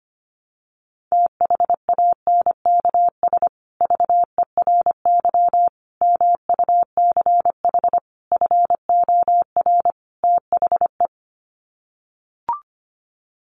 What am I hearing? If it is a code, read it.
Morse code, heard as a keyed tone, about 25 wpm, 700 Hz. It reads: THANKS VERY MUCH FOR THE
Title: Sets of 5 Words Encore2 25wpm